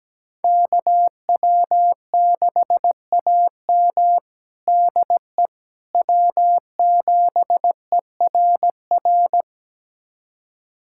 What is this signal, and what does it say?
Morse code, heard as a keyed tone, about 17 wpm, 700 Hz. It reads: KW6AM DE W7ERR
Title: USA Callsign DE Callsign - 5 Characters - 17wpm